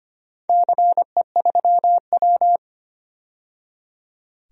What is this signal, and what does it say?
Morse code, heard as a keyed tone, about 25 wpm, 700 Hz. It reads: CE3W
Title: DXCC-100 Call Signs - 4 Characters - 25wpm